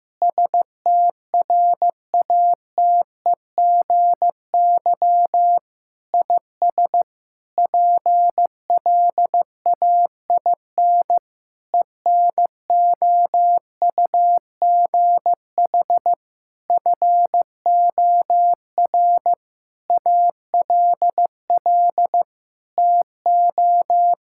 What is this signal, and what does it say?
Morse code, heard as a keyed tone, about 15 wpm, 700 Hz. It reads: STRATEGY IS PLAIN ENOUGH FOR ALL TO
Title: Sentences from Top 1000 Words - Review Part 5 - 15wpm